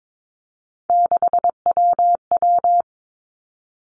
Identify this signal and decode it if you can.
Morse code, heard as a keyed tone, about 22 wpm, 700 Hz. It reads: 6WW
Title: Three Challenging Letter-Number Combinations 22wpm